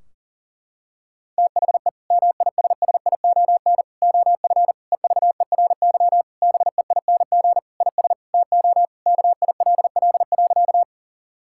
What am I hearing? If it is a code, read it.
Morse code, heard as a keyed tone, about 40 wpm, 700 Hz. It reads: THE MISSION OF EVERY BEING IS TO KILL.
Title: Sentences from Top 1600 Words 40wpm